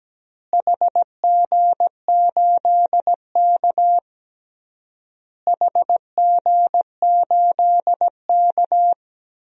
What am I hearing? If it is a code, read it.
Morse code, heard as a keyed tone, about 17 wpm, 700 Hz. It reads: HG8K HG8K
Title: DXCC-100 Call Signs - 4 Characters - Repeated 3x - 17wpm